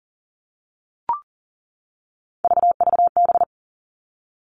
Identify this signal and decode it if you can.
Morse code, heard as a keyed tone, about 41 wpm, 700 Hz. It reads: VVB
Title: Three Challenging Letter-Number Combinations 40wpm